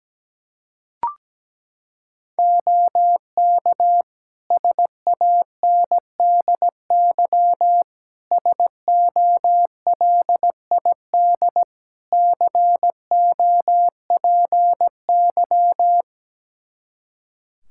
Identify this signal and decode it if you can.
Morse code, heard as a keyed tone, about 17 wpm, 700 Hz. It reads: OK SANDY SOLID COPY
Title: K1USN SST QSO Practice 17wpm